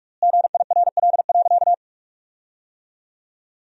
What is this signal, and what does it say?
Morse code, heard as a keyed tone, about 41 wpm, 700 Hz. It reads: GIRL.
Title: 3x Word-Spacing - Sentences from Top 400 Words - 45wpm